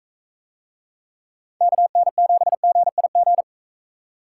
Morse code, heard as a keyed tone, about 40 wpm, 700 Hz. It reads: KN7GIG